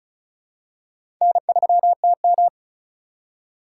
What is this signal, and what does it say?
Morse code, heard as a keyed tone, about 36 wpm, 700 Hz. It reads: N3TM
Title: USA Call Signs - 4 Characters - 3x Character Spacing - 35wpm